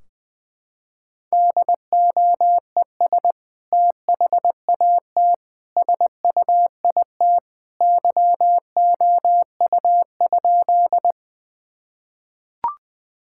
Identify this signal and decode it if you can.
Morse code, heard as a keyed tone, about 20 wpm, 700 Hz. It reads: DOES THAT SUIT YOU?
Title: Sentences from Top 1600 Words 20wpm